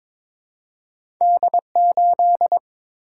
Morse code, heard as a keyed tone, about 22 wpm, 700 Hz. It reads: D8
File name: practice / 22wpm / Two Letter-Number Combinations - 22wpm